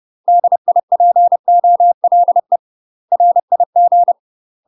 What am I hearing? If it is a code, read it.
Morse code, heard as a keyed tone, about 30 wpm, 700 Hz. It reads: DIPOLE RIG
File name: UK MASTER QSOs TRAINING / 30WPM / UK MASTER QSOs 0044